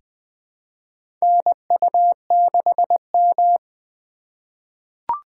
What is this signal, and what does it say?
Morse code, heard as a keyed tone, about 20 wpm, 700 Hz. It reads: NU6M
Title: USA Call Signs - 2x Character Spacing - 20wpm